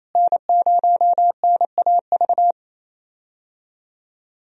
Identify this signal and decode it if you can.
Morse code, heard as a keyed tone, about 28 wpm, 700 Hz. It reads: N0NAV
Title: USA Call Signs - 5 Characters - Repeated 3x - 28wpm